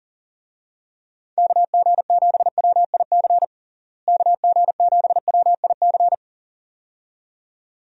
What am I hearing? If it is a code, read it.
Morse code, heard as a keyed tone, about 40 wpm, 700 Hz. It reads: KG7WIC KG7WIC
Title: USA Call Signs - 6 Characters - Repeated 2x - 40wpm